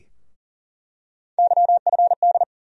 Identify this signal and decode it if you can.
Morse code, heard as a keyed tone, about 40 wpm, 700 Hz. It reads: YFD
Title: USA Call Sign Postfixes - 3 Letters 40wpm